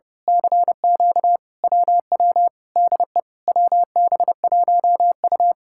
Morse code, heard as a keyed tone, about 28 wpm, 700 Hz. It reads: CQ WW DE WB1U
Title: Speed-Racing Common QSO Phrases Encore - 30wpm